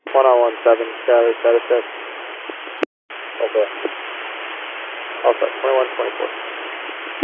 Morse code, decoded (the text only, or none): none